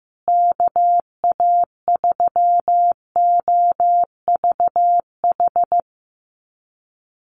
Morse code, heard as a keyed tone, about 15 wpm, 700 Hz. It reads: KA3OVH
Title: USA Call Signs Repeated 2x 15wpm